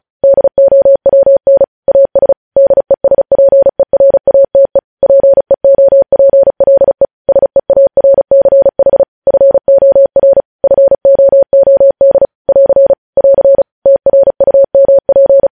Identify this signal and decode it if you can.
Morse code, heard as a keyed tone, about 36 wpm, 550 Hz. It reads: DOWN AS DESPERATE PEOPLE SEARCH FOR FOOD <AR> <AR> TRUMP
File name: podcast / News headlines-1x-35x35